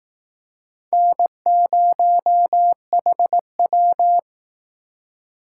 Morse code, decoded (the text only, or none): N0HW